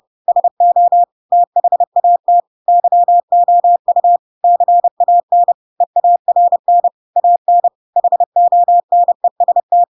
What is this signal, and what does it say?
Morse code, heard as a keyed tone, about 30 wpm, 700 Hz. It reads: SO THAT YOU CAN EARN AN HONEST
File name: Pinocchio / Pinocchio ch4 30 30 ik1hge